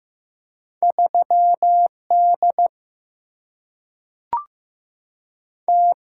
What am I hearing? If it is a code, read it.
Morse code, heard as a keyed tone, about 15 wpm, 700 Hz. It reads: 3 D T
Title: Single-Letter Single-Number 15wpm